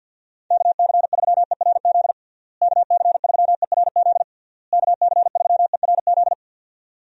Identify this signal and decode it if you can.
Morse code, heard as a keyed tone, about 51 wpm, 700 Hz. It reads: KC3ERZ KC3ERZ KC3ERZ
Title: USA Call Signs - 6 Characters - Repeated 3x - 50wpm